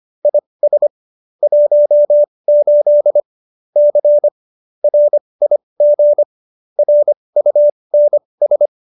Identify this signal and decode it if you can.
Morse code, heard as a keyed tone, about 25 wpm, spaced as for 20 wpm, 590 Hz. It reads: IS 18 C RIG RUNS
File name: qso208b 20